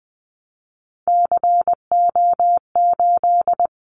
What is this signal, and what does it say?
Morse code, heard as a keyed tone, about 20 wpm, 700 Hz. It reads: CO8
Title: Callsign Prefixes for Top 100 DXCC - 20wpm